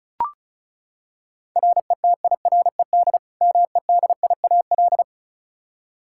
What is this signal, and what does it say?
Morse code, heard as a keyed tone, about 36 wpm, 700 Hz. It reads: RETIRED MEDIAL
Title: Practice QSO - Exchange 3 - More Info - 35wpm